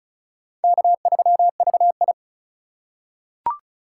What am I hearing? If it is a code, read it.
Morse code, heard as a keyed tone, about 36 wpm, 700 Hz. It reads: K3VI
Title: USA Call Signs - 4 Characters - 2x Character Spacing - 35wpm